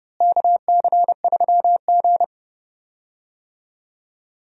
Morse code, heard as a keyed tone, about 30 wpm, 700 Hz. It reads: KC3G
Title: USA Call Signs - 4 Characters - Repeated 3x - 30wpm